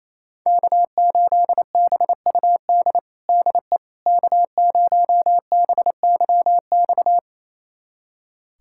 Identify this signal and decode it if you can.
Morse code, heard as a keyed tone, about 28 wpm, 700 Hz. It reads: K8BUD DE K0BYX
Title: USA Callsign DE Callsign - 5 Characters - 28wpm